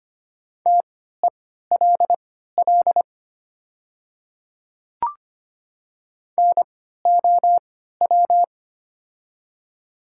Morse code, heard as a keyed tone, about 25 wpm, spaced as for 14 wpm, 700 Hz. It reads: TELL NOW
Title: 3x Character Spacing - Top 200 Words - 25wpm